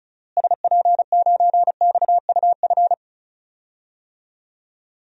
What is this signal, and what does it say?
Morse code, heard as a keyed tone, about 34 wpm, 700 Hz. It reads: SP9XUF
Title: DXCC-100 Call Signs - 6 Characters - Repeated 2x - 35wpm